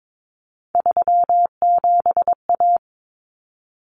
Morse code, heard as a keyed tone, about 22 wpm, 700 Hz. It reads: 37A
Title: Common Cut Numbers - 3-Digit 22wpm